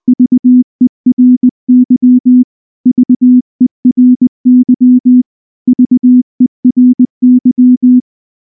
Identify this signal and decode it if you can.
Morse code, heard as a keyed tone, about 19 wpm, 250 Hz. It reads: VERY VERY VERY
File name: morse / morse-training-20WPM-1